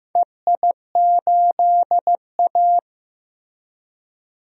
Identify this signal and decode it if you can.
Morse code, heard as a keyed tone, about 16 wpm, 700 Hz. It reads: EI8A
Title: DXCC-100 Call Signs - 4 Characters - Repeated 2x - 15wpm